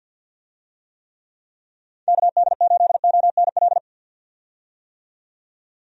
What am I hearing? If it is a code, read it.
Morse code, heard as a keyed tone, about 52 wpm, 700 Hz. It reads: KD8ONL
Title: USA Call Signs - 6 Characters - 50wpm